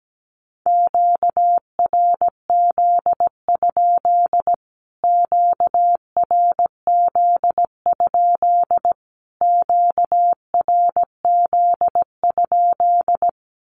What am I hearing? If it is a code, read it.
Morse code, heard as a keyed tone, about 17 wpm, 700 Hz. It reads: QRZ? QRZ? QRZ?
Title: Q-Codes Repeated 3x 17wpm